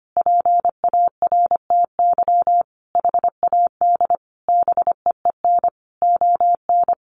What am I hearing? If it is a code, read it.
Morse code, heard as a keyed tone, about 25 wpm, 700 Hz. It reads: PARTY HAD BEEN ON
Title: Sentences from Top 1400 Words 25wpm